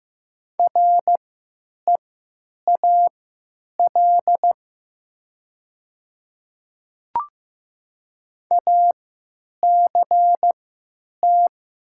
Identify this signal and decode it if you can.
Morse code, heard as a keyed tone, about 15 wpm, spaced as for 9 wpm, 700 Hz. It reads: REAL ACT
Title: 3x Character Spacing - Top 300 Words - 15wpm